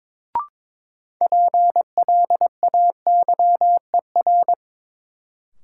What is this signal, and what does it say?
Morse code, heard as a keyed tone, about 23 wpm, 700 Hz. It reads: PLAYER
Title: Sentences from Top 1000 Words - Review Part 4 - New Word Drill - 22wpm